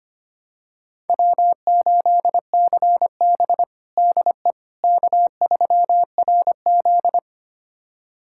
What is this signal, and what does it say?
Morse code, heard as a keyed tone, about 25 wpm, 700 Hz. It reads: W8CB DE K3RZ
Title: USA Callsign DE Callsign - 4 Characters - 25wpm